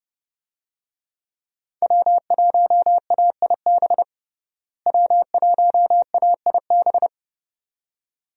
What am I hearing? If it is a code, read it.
Morse code, heard as a keyed tone, about 32 wpm, 700 Hz. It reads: W1AIB W1AIB
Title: USA Call Signs - 5 Characters - Repeated 2x - 30wpm